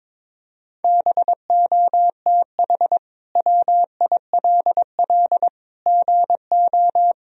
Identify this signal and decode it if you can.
Morse code, heard as a keyed tone, about 22 wpm, 700 Hz. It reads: BOTH WILL GO
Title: Sentences from Top 1000 Words - Review Part 2 - 22wpm